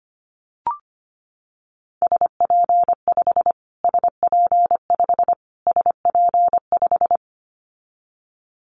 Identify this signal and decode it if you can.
Morse code, heard as a keyed tone, about 25 wpm, 700 Hz. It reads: SP5 SP5 SP5
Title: Callsign Prefixes for Top 25 DXCC - Repeated 3x - 25wpm